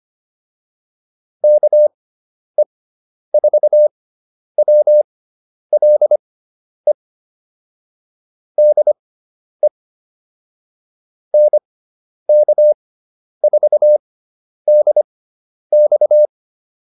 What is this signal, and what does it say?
Morse code, heard as a keyed tone, about 25 wpm, spaced as for 10 wpm, 600 Hz. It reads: KE4WLE DE NK4DX